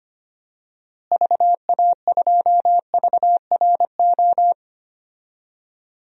Morse code, heard as a keyed tone, about 25 wpm, 700 Hz. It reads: VA2VRO
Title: DXCC-100 Call Signs - 6 Characters - Repeated 2x - 25wpm